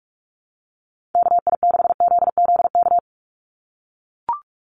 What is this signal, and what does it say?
Morse code, heard as a keyed tone, about 44 wpm, 700 Hz. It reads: KI6ZZK
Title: USA Call Signs 45wpm